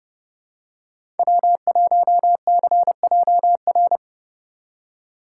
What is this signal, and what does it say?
Morse code, heard as a keyed tone, about 30 wpm, 700 Hz. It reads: W1CJR
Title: USA Call Signs - 5 Characters - 4x Character Spacing - 30wpm